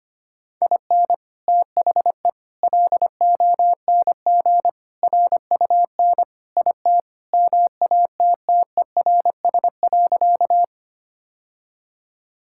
Morse code, heard as a keyed tone, about 25 wpm, 700 Hz. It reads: IN THE LONG RUN IT MATTERS.
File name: practice / 25wpm / Sentences with 4-Word Phrases - 25wpm